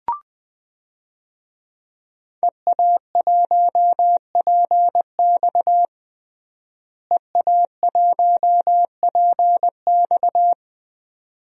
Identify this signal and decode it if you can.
Morse code, heard as a keyed tone, about 20 wpm, 700 Hz. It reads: EA1PX EA1PX
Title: DXCC-100 Call Signs - 5 Characters - Repeated 2x - 20wpm